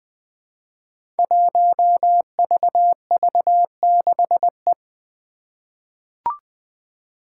Morse code, heard as a keyed tone, about 20 wpm, 700 Hz. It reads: 1VV6E
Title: Five Challenging Letter-Number Combinations 20wpm